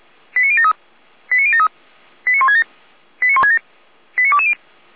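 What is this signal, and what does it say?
Five-tone selective call (ZVEI), heard as quick runs of five tones, tones of about 70 ms. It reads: 89073 89073 89167 89167 89220
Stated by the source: ZVEI five-tone selective call received by SDR, from the Feuerwehr Orlamünde (fire brigade) audio archive.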